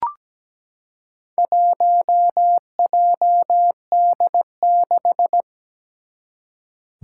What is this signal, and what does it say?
Morse code, heard as a keyed tone, about 17 wpm, 700 Hz. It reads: 1JD6
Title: Four Challenging Letter-Number Combinations 17wpm